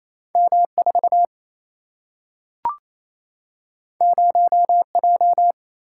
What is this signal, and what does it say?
Morse code, heard as a keyed tone, about 28 wpm, 700 Hz. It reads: M4 0J
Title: Single-Letter Single-Number 28wpm